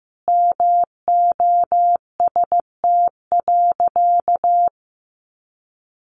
Morse code, heard as a keyed tone, about 14 wpm, 700 Hz. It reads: MOST.
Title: Sentences from Top 1400 Words 15wpm